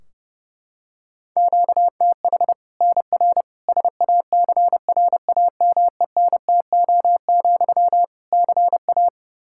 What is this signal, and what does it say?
Morse code, heard as a keyed tone, about 30 wpm, 700 Hz. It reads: QTH NR SACRAMENTO, CA
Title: Practice QSO - Exchange 1 - Beginning - 30wpm